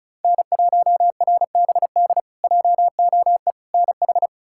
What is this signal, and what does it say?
Morse code, heard as a keyed tone, about 36 wpm, 700 Hz. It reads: N1RBD JOE NH
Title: K1USN SST QSO Practice - 35wpm